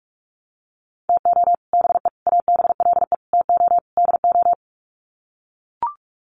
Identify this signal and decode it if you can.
Morse code, heard as a keyed tone, about 44 wpm, 700 Hz. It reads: TO BE ABLE TO DO
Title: Sets of 5 Words 45wpm